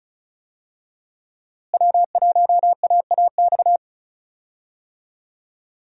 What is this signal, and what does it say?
Morse code, heard as a keyed tone, about 36 wpm, 700 Hz. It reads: W1AAX
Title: USA Call Signs - 5 Characters - Repeated 2x - 35wpm